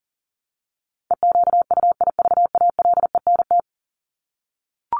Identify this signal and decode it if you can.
Morse code, heard as a keyed tone, about 40 wpm, 700 Hz. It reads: EQUIVALENT
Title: Top 1501-1750 Words 40wpm